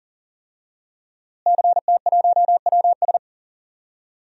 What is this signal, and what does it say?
Morse code, heard as a keyed tone, about 40 wpm, 700 Hz. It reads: CT1WS
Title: DXCC-100 Call Signs - 5 Characters - Repeated 2x - 40wpm